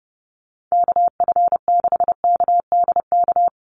Morse code, heard as a keyed tone, about 28 wpm, 700 Hz. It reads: KF6KDK
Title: USA Call Signs Repeated 3x 30wpm